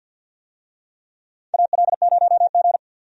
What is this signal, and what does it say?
Morse code, heard as a keyed tone, about 52 wpm, 700 Hz. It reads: AL0G
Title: USA Call Signs - 4 Characters - Repeated 2x - 50wpm